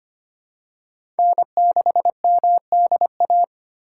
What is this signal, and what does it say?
Morse code, heard as a keyed tone, about 25 wpm, 700 Hz. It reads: N6MDA
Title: USA Call Signs - 5 Characters - 2x Character Spacing - 25wpm